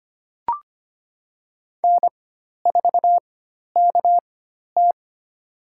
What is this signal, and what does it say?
Morse code, heard as a keyed tone, about 25 wpm, spaced as for 12 wpm, 700 Hz. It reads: N4KT
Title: USA Call Signs - 4 Characters - 4x Character Spacing - 25wpm